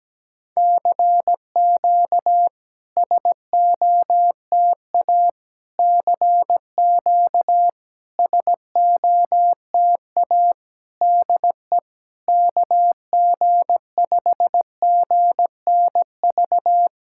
Morse code, heard as a keyed tone, about 17 wpm, 700 Hz. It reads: CQ SOTA CQ SOTA DE KG5GNV
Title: SOTA QSO Practice 17wpm